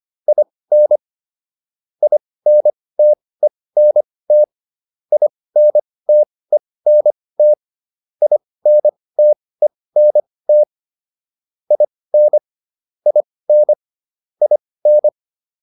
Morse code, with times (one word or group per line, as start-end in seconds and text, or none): 0.3-1.0 IN
2.0-4.4 INTENT
5.1-7.5 INTENT
8.2-10.6 INTENT
11.7-12.4 IN
13.1-13.7 IN
14.4-15.1 IN